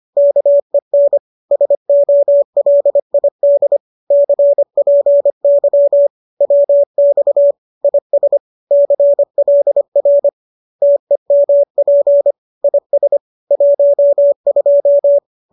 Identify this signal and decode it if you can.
Morse code, heard as a keyed tone, about 25 wpm, 570 Hz. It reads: KEN SOLID CPY WX IS CLR TEMP IS 12
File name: qso205 25